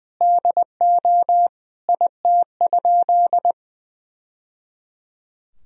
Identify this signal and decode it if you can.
Morse code, heard as a keyed tone, about 20 wpm, 700 Hz. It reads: DO IT?
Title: Sentences from Top 1000 Words - Review Part 2 - 20wpm